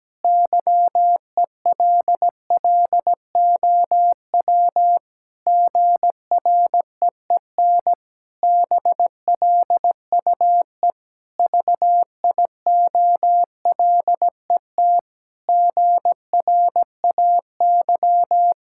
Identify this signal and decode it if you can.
Morse code, heard as a keyed tone, about 17 wpm, 700 Hz. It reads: YELLOW GREEN BLUE VIOLET GRAY
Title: Pangram Challenge v2 - 17wpm